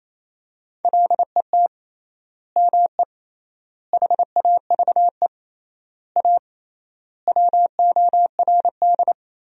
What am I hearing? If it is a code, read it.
Morse code, heard as a keyed tone, about 27 wpm, 700 Hz. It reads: LET ME HAVE A WORD
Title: 3x Word-Spacing - Sentences from Top 300 Words - 28wpm